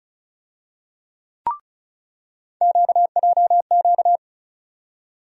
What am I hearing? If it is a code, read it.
Morse code, heard as a keyed tone, about 34 wpm, 700 Hz. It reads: QJQ
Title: USA Call Sign Postfixes - 3 Letters 35wpm